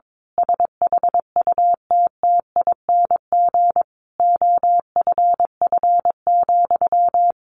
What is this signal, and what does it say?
Morse code, heard as a keyed tone, about 22 wpm, 700 Hz. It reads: SHUTTING OFF,
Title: A Sound Of Thunder 22wpm